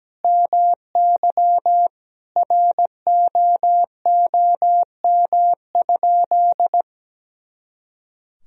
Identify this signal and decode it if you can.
Morse code, heard as a keyed tone, about 17 wpm, 700 Hz. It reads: MY ROOM?
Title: Sentences from Top 300 Words and Abbreviations 17wpm